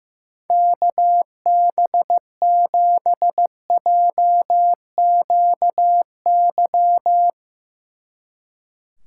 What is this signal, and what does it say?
Morse code, heard as a keyed tone, about 16 wpm, 700 Hz. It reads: KB7JQY
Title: Speed Racing - USA 2x3 Call Signs 15wpm